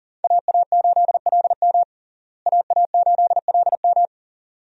Morse code, heard as a keyed tone, about 37 wpm, 700 Hz. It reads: AA8LM AA8LM
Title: USA Call Signs - 5 Characters - Repeated 2x - 40wpm